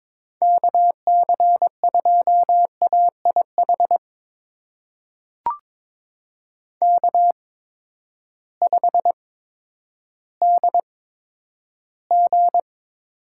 Morse code, heard as a keyed tone, about 22 wpm, spaced as for 11 wpm, 700 Hz. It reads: KC2AIH K5DG
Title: USA Call Signs - 8x Character Spacing - 22wpm